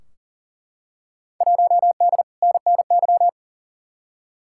Morse code, heard as a keyed tone, about 38 wpm, 700 Hz. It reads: 1D NNY
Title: Field Day Exchanges - 40wpm